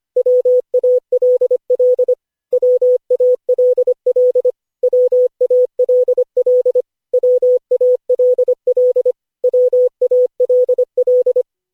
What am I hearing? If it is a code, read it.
Morse code, heard as a keyed tone, about 25 wpm, 500 Hz. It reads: WALL WALL WALL WALL WALL
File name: cw / Session5 cw training